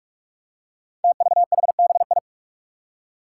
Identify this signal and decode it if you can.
Morse code, heard as a keyed tone, about 44 wpm, 700 Hz. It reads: TVHBI
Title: Five Challenging Letter-Number Combinations 45wpm